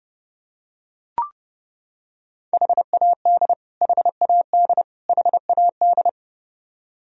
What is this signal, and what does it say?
Morse code, heard as a keyed tone, about 30 wpm, 700 Hz. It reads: HAD HAD HAD
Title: Top 100 Words Repeated 30wpm